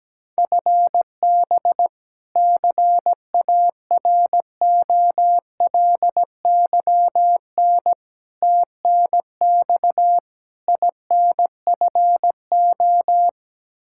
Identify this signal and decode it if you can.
Morse code, heard as a keyed tone, about 17 wpm, 700 Hz. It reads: FB CAROLYN TNX INFO
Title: Practice QSO - Exchange 3 - More Info - 17wpm